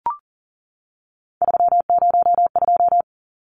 Morse code, heard as a keyed tone, about 40 wpm, 700 Hz. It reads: 302
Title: Numbers - 3-Digit 40wpm